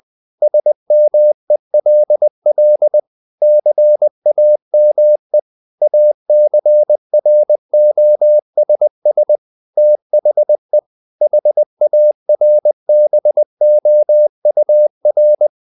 Morse code, heard as a keyed tone, about 20 wpm, 600 Hz. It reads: SMELL CAME ACROSS THE HARBOUR
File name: the old man and the sea / the old man and the sea part 1 20wpm